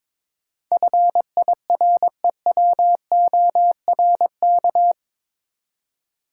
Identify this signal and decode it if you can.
Morse code, heard as a keyed tone, about 22 wpm, 700 Hz. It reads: FIREWORK
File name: practice / 22wpm / Eight Letter Words - 22wpm